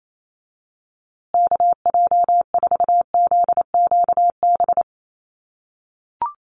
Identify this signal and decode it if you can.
Morse code, heard as a keyed tone, about 28 wpm, 700 Hz. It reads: KJ4ZQB
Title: USA Call Signs Repeated 2x 28wpm